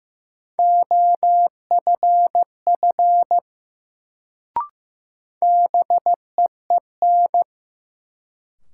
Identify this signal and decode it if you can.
Morse code, heard as a keyed tone, about 15 wpm, 700 Hz. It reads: OFF BEEN
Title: Sets of 2 Words - New Word Drill - 15wpm